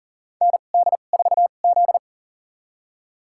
Morse code, heard as a keyed tone, about 40 wpm, spaced as for 29 wpm, 700 Hz. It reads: ND4Z
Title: USA Call Signs - 4 Characters - 2x Character Spacing - 40wpm